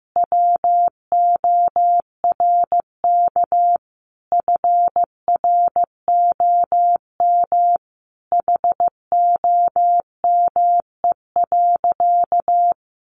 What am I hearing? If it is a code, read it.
Morse code, heard as a keyed tone, about 15 wpm, 700 Hz. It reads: WORK FROM HOME.
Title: Sentences from Top 1500 Words 15wpm